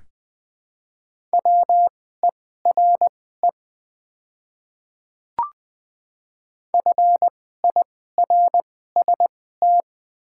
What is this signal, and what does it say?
Morse code, heard as a keyed tone, about 20 wpm, spaced as for 15 wpm, 700 Hz. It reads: WERE FIRST
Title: 2x Character Spacing - Top 200 Words - 20wpm